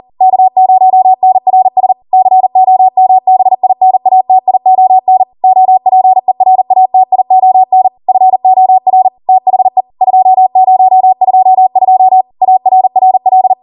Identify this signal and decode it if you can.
Morse code, heard as a keyed tone, about 40 wpm, 750 Hz. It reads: K0NRS COMBINATION OPERATION FOR THE 2022 ARRL